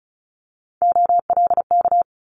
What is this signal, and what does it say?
Morse code, heard as a keyed tone, about 36 wpm, 700 Hz. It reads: OLK
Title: Multi-letter Phonemes Quick Fire 35wpm